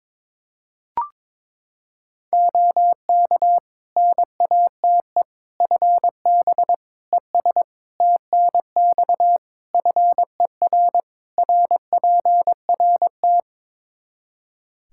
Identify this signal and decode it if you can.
Morse code, heard as a keyed tone, about 21 wpm, 700 Hz. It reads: OK NATE FB ES TNX FER RPRT
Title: Practice QSO - Exchange 2 - Info - 22wpm